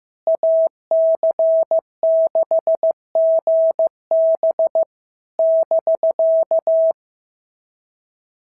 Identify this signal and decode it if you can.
Morse code, heard as a keyed tone, about 15 wpm, 650 Hz. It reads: AC6GB <BK>
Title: SOTA QSO Practice 15wpm